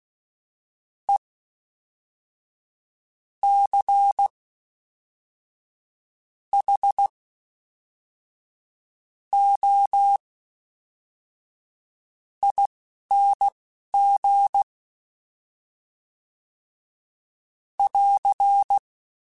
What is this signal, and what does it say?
Morse code, heard as a keyed tone, about 16 wpm, spaced as for 11 wpm, 800 Hz. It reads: E C H O ING <AR>
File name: Special INGs / Special INGs-mp3s / Fast ing @10